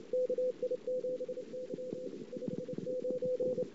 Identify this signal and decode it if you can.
Morse code, heard as a keyed tone, about 23 wpm, 500 Hz. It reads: KI7Y 59
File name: CW / 2008-ss-cw-k5tr-op-n5rz -19:00